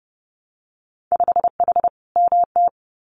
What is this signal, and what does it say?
Morse code, heard as a keyed tone, about 30 wpm, 700 Hz. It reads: 5H MT
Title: Winter Field Day Exchanges Encore 30wpm